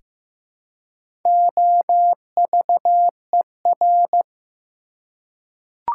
Morse code, heard as a keyed tone, about 15 wpm, 700 Hz. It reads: OVER
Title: Prefixes and Suffixes 15wpm